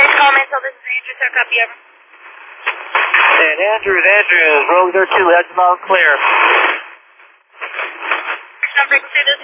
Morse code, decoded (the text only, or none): none